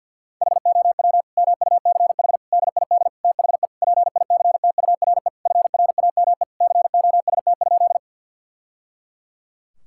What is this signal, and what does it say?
Morse code, heard as a keyed tone, about 50 wpm, 700 Hz. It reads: HOW MUCH DID THE PICTURE FRAME COST?
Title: Sentences from Top 1000 Words - Review Part 6 - 50wpm